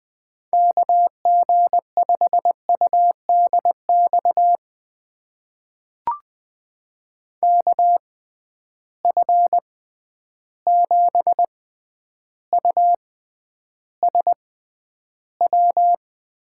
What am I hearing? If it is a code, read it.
Morse code, heard as a keyed tone, about 20 wpm, spaced as for 10 wpm, 700 Hz. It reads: KG5UDX KF7USW
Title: USA Call Signs - 6 Characters - 6x Character Spacing - 20wpm